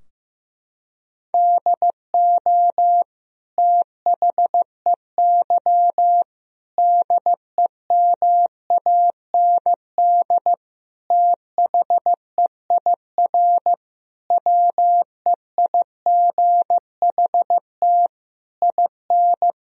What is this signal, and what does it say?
Morse code, heard as a keyed tone, about 15 wpm, 700 Hz. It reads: DO THEY DEMAND THEIR WEIGHT IN
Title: Sentences from Top 1000 Words - Review Part 3 - 15wpm